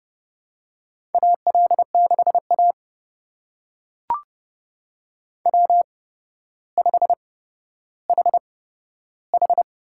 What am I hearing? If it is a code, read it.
Morse code, heard as a keyed tone, about 30 wpm, spaced as for 13 wpm, 700 Hz. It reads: AL6A W5HH
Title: USA Call Signs - 4 Characters - 8x Character Spacing - 30wpm